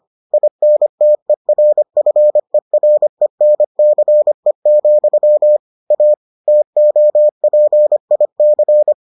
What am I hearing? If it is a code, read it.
Morse code, heard as a keyed tone, about 25 wpm, 600 Hz. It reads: INTERFERENCE, A TOPIC